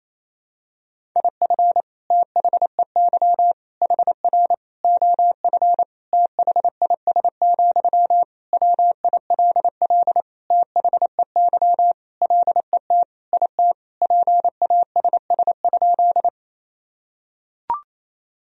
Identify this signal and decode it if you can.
Morse code, heard as a keyed tone, about 28 wpm, 700 Hz. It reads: IF THEY HR OF THIS, WILL THEY LET IT PASS?
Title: Sentences from Top 400 Words and Abbreviations 28wpm